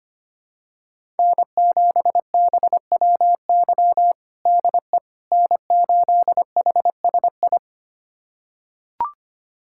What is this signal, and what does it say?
Morse code, heard as a keyed tone, about 25 wpm, 700 Hz. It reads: N7BWY DE N8HSI
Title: USA Callsign DE Callsign - 5 Characters - 25wpm